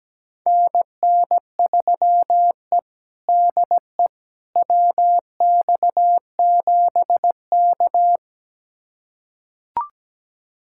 Morse code, heard as a keyed tone, about 17 wpm, 700 Hz. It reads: NN3E DE WX7K
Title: USA Callsign DE Callsign - 4 Characters - 17wpm